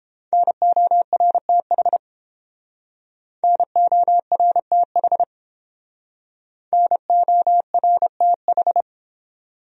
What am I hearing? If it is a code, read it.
Morse code, heard as a keyed tone, about 29 wpm, 700 Hz. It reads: NORTH NORTH NORTH